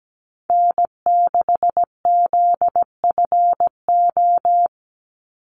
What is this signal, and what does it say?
Morse code, heard as a keyed tone, about 17 wpm, 700 Hz. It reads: N6ZFO
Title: Call Signs - 17wpm